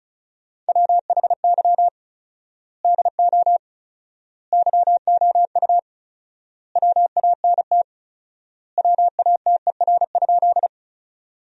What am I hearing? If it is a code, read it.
Morse code, heard as a keyed tone, about 36 wpm, 700 Hz. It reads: WHY DO YOU WANT WATER?
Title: 4x Word-Spacing - Sentences from Top 200 Words - 35wpm